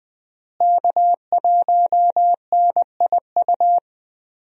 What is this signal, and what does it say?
Morse code, heard as a keyed tone, about 20 wpm, 700 Hz. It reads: K1NIU
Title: USA Call Signs - 5 Characters - Repeated 2x - 20wpm